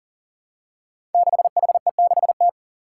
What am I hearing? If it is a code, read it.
Morse code, heard as a keyed tone, about 40 wpm, 700 Hz. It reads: 6HE6T
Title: Five Challenging Letter-Number Combinations 40wpm